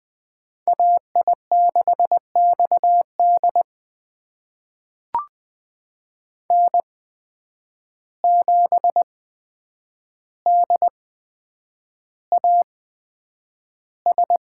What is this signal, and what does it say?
Morse code, heard as a keyed tone, about 20 wpm, spaced as for 9 wpm, 700 Hz. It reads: AI6XD N7DAS
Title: USA Call Signs - 5 Characters - 8x Character Spacing - 20wpm